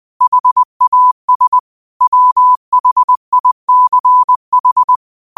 Morse code, heard as a keyed tone, about 20 wpm, 1000 Hz. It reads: HAS WHICH
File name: qotdfiles / 20wpm